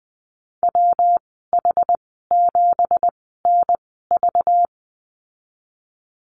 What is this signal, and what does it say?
Morse code, heard as a keyed tone, about 20 wpm, spaced as for 15 wpm, 700 Hz. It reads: WH7NV
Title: 231 Lesson 37g - Letter x - Call Signs 20wpm